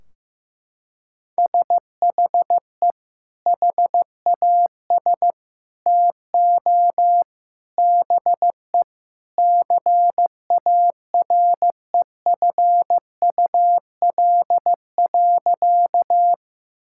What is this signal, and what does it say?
Morse code, heard as a keyed tone, about 15 wpm, 700 Hz. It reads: SHE HAS TO BE CAREFUL.
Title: Sentences from Top 1600 Words 15wpm